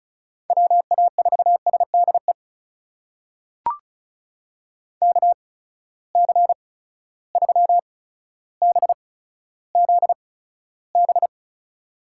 Morse code, heard as a keyed tone, about 36 wpm, spaced as for 15 wpm, 700 Hz. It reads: WA4SDE KC3BZB
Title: USA Call Signs - 6 Characters - 8x Character Spacing - 35wpm